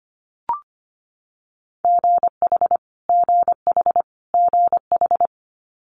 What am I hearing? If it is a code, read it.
Morse code, heard as a keyed tone, about 25 wpm, 700 Hz. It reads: GH GH GH
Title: Multi-letter Phonemes Repeated 3x 25wpm